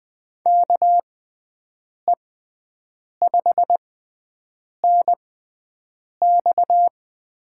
Morse code, heard as a keyed tone, about 20 wpm, spaced as for 7 wpm, 700 Hz. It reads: KE5NX
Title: USA Call Signs - 6 Characters - 6x Character Spacing - 20wpm